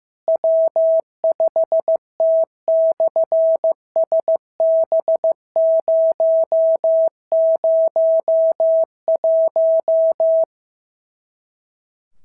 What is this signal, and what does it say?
Morse code, heard as a keyed tone, about 15 wpm, 650 Hz. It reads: W5T/SB001
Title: SOTA QSO Practice 15wpm